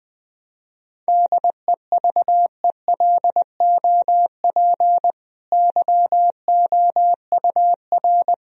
Morse code, heard as a keyed tone, about 20 wpm, 700 Hz. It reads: DEVELOP YOUR
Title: Sentences from Top 1000 Words - Review Part 2 - 20wpm